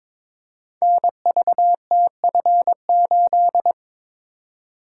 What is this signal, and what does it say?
Morse code, heard as a keyed tone, about 22 wpm, 700 Hz. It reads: NVTF8
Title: Code Groups - 5 Letter:Numbers - 22wpm